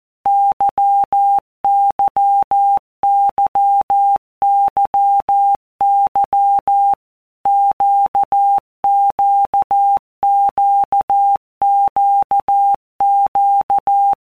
Morse code, heard as a keyed tone, about 14 wpm, 800 Hz. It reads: YYYYY QQQQQ
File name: Basic Letter Groups By WPM's / Groups@13 / LFCYQ@13